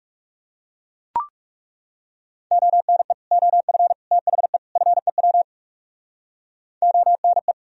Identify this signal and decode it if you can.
Morse code, heard as a keyed tone, about 37 wpm, 700 Hz. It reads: ONE OF THE FEW ONE
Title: Speed-Racing - Sets of 4 Words 30wpm